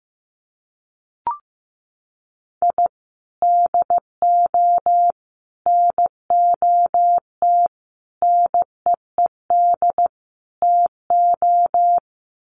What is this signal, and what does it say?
Morse code, heard as a keyed tone, about 15 wpm, 700 Hz. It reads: I DO NOT NEED TO
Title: Sets of 5 Words 15wpm